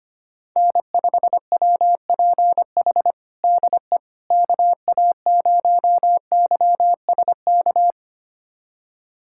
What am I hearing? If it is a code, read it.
Morse code, heard as a keyed tone, about 25 wpm, 700 Hz. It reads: N5WPH DE KA0YSK
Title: USA Callsign DE Callsign 25wpm